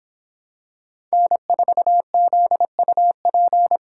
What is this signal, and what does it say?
Morse code, heard as a keyed tone, about 26 wpm, 700 Hz. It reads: N4ZUP